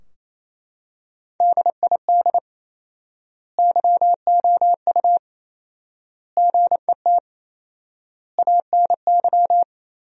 Morse code, heard as a keyed tone, about 28 wpm, 700 Hz. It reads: DID YOU GET ANY
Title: 4x Word-Spacing - Sentences from Top 400 Words - 28wpm